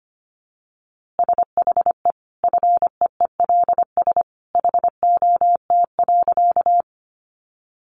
Morse code, heard as a keyed tone, about 25 wpm, 700 Hz. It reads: SHE FEELS HOT.
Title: Easy Phrases 25wpm